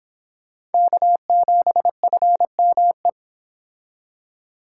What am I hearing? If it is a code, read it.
Morse code, heard as a keyed tone, about 28 wpm, 700 Hz. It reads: K7FME